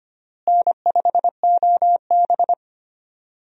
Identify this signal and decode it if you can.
Morse code, heard as a keyed tone, about 25 wpm, 700 Hz. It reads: N5OB
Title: USA Call Signs - 4 Characters - Repeated 2x - 25wpm